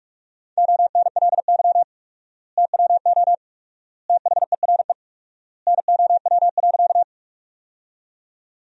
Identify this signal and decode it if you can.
Morse code, heard as a keyed tone, about 44 wpm, 700 Hz. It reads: ONLY TWO THERE NOW.
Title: 4x Word-Spacing - Sentences from Top 200 Words - 45wpm